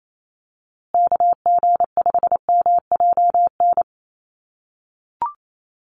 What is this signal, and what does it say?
Morse code, heard as a keyed tone, about 28 wpm, 700 Hz. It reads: KG5MJN
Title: USA Call Signs Repeated 2x 28wpm